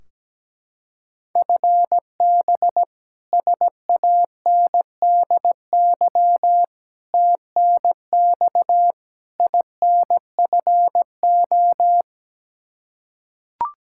Morse code, heard as a keyed tone, about 17 wpm, 700 Hz. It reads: FB SANDY TNX INFO
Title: Practice QSO - Exchange 3 - More Info - 17wpm